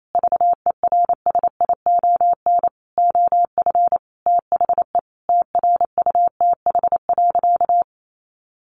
Morse code, heard as a keyed tone, about 28 wpm, 700 Hz. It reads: VERSION OF THE TRUTH.
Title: Sentences from Top 1400 Words 28wpm